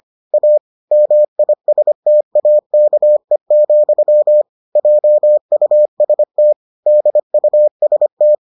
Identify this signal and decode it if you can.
Morse code, heard as a keyed tone, about 25 wpm, 600 Hz. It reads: A MISTAKE, JUST DUST